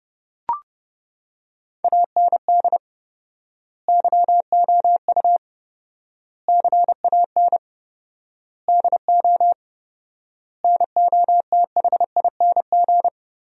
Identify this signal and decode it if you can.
Morse code, heard as a keyed tone, about 29 wpm, 700 Hz. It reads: AND YOU CAN DO NOTHING
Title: 4x Word-Spacing - Sentences from Top 500 Words - 30wpm